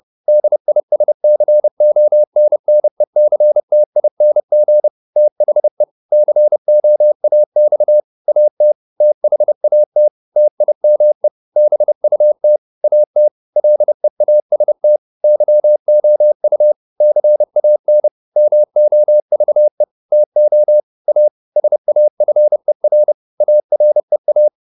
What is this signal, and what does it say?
Morse code, heard as a keyed tone, about 30 wpm, 600 Hz. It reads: DISCONNECTING THE COAX AT THAT TIME BUT AT LEAST YOU CAN MOVE TO A SAFER AREA